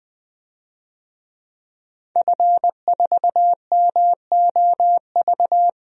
Morse code, heard as a keyed tone, about 21 wpm, 700 Hz. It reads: F4MOV
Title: DXCC-100 Call Signs - 5 Characters - 20wpm